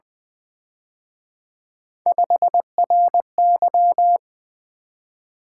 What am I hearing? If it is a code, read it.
Morse code, heard as a keyed tone, about 24 wpm, 700 Hz. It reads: 5RY
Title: Speed-Racing - Three Letter-Number Combinations - 20wpm